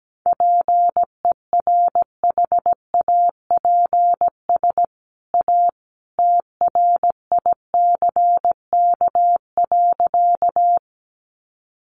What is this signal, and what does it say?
Morse code, heard as a keyed tone, about 17 wpm, 700 Hz. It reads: PERHAPS A TRICK.
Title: Sentences from Top 1500 Words 17wpm